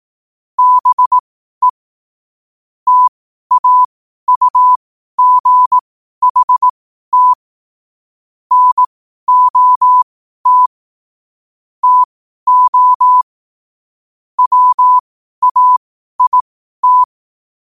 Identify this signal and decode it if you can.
Morse code, heard as a keyed tone, about 18 wpm, spaced as for 12 wpm, 1000 Hz. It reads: BE TAUGHT NOT TO WAIT
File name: qotdfiles / 12wpm